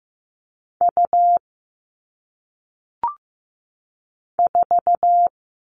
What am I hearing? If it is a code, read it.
Morse code, heard as a keyed tone, about 15 wpm, 700 Hz. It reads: U 4
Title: All Cut Numbers - 1-Digit 15wpm